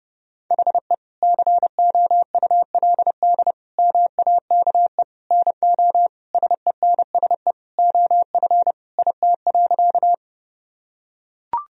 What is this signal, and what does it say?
Morse code, heard as a keyed tone, about 32 wpm, 700 Hz. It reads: HE COULD MAKE NO SENSE OF IT.
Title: Sentences from Top 1000 Words - Review Part 4 - 30wpm